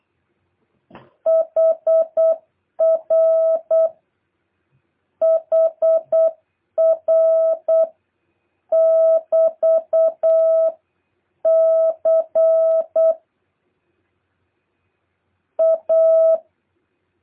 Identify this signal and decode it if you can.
Morse code, heard as a keyed tone, about 8 wpm, 650 Hz. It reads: HR HR <BT> C A
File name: jh8cbh / 2023cw